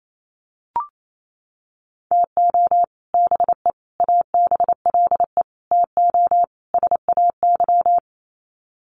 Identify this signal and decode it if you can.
Morse code, heard as a keyed tone, about 28 wpm, 700 Hz. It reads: TO BE ABLE TO SAY
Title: Sets of 5 Words Encore2 28wpm